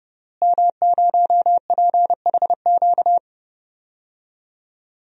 Morse code, heard as a keyed tone, about 32 wpm, 700 Hz. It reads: M0PHQ
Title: DXCC-100 Call Signs - 5 Characters - Repeated 3x - 30wpm